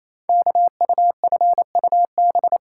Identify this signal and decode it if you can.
Morse code, heard as a keyed tone, about 29 wpm, 700 Hz. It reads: KUFUB
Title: Code Groups - 5 Letters - 28wpm